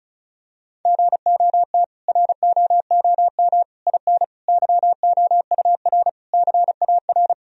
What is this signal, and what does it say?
Morse code, heard as a keyed tone, about 36 wpm, 700 Hz. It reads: GOT ROOM IN YOUR CAR
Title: Sentences from Top 1000 Words - Review Part 1 - 35wpm